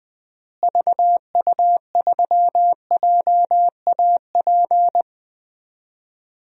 Morse code, heard as a keyed tone, about 20 wpm, 700 Hz. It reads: VU3JAP
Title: DXCC-100 Call Signs - 6 Characters - Repeated 3x - 20wpm